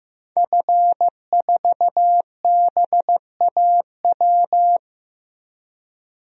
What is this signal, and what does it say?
Morse code, heard as a keyed tone, about 15 wpm, 700 Hz. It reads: F4BAW
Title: DXCC-100 Call Signs - 5 Characters - Repeated 3x - 15wpm